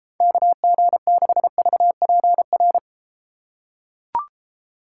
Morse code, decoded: KG6VPR

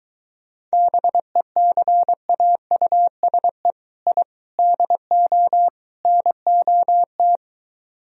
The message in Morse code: BECAUSE I DO NOT

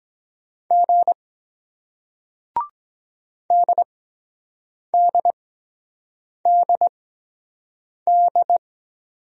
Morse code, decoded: G DDDD

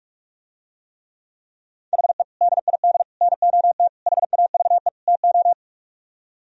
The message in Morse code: HE DID NOT HAVE TO

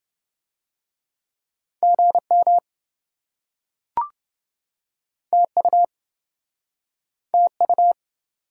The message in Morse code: GM TU TU

